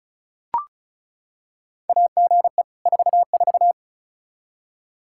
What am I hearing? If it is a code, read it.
Morse code, heard as a keyed tone, about 36 wpm, 700 Hz. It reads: AGE 44
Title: Practice QSO - Exchange 3 - More Info - 35wpm